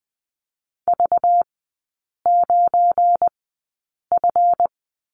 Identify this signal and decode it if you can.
Morse code, heard as a keyed tone, about 20 wpm, spaced as for 8 wpm, 700 Hz. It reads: V9F
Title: 198 Lesson 32e - Letter k - Three Characters 20wpm